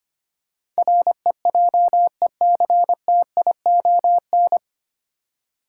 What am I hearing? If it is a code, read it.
Morse code, heard as a keyed tone, about 25 wpm, 700 Hz. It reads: REJECTION
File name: Nine Letter Words - 25wpm